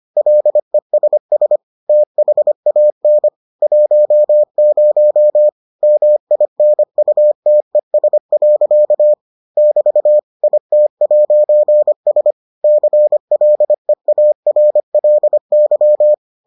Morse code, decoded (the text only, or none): LESS THAN 10 MINUTES. <BT> IT'S CLEARLY